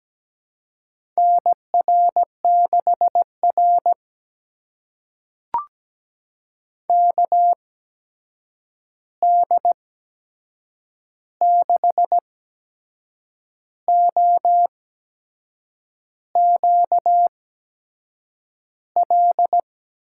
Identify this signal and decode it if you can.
Morse code, heard as a keyed tone, about 17 wpm, spaced as for 6 wpm, 700 Hz. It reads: NR6R KD6OQL
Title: USA Call Signs - 8x Character Spacing - 17wpm